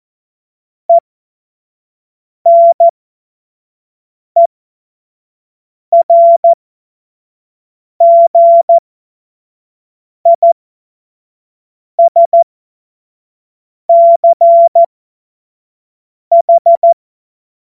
Morse code, heard as a keyed tone, about 14 wpm, spaced as for 5 wpm, 680 Hz. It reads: ENERGISCH
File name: Morsekurs / Witze / Witz 04. Januar 2022, 25 F70